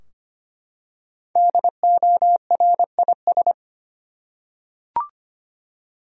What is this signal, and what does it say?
Morse code, heard as a keyed tone, about 25 wpm, 700 Hz. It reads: DORIS
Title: USA Common Names - Max 5 Letters - 25wpm